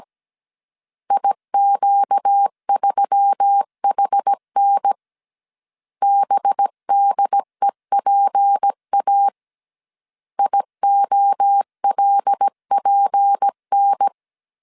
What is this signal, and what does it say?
Morse code, heard as a keyed tone, about 17 wpm, 800 Hz. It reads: IQ3HN BDEPA IOLPN
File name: CW / DET / 1 6 1500C S C 16wpm 2733